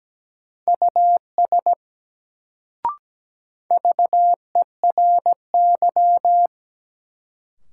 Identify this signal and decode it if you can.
Morse code, heard as a keyed tone, about 17 wpm, 700 Hz. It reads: US VERY
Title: Sets of 2 Words - New Word Drill - 17wpm